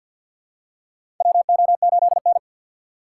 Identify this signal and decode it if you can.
Morse code, heard as a keyed tone, about 53 wpm, 700 Hz. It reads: WO8N
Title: USA Call Signs - 2x Character Spacing - 50wpm